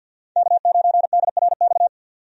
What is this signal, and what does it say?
Morse code, heard as a keyed tone, about 53 wpm, 700 Hz. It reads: K9DRX
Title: USA Call Signs - 5 Characters - 50wpm